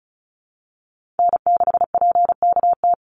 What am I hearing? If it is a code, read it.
Morse code, heard as a keyed tone, about 34 wpm, 700 Hz. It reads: N6PKT
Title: NA SOTA Chaser Callsigns 35wpm